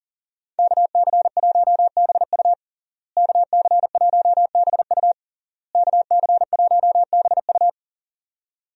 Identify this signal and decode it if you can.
Morse code, heard as a keyed tone, about 40 wpm, 700 Hz. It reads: KC1BU KC1BU KC1BU
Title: USA Call Signs - 5 Characters - Repeated 3x - 40wpm